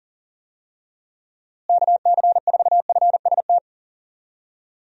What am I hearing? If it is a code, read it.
Morse code, heard as a keyed tone, about 37 wpm, 700 Hz. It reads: KC4FST
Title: USA Call Signs - 6 Characters - Repeated 2x - 40wpm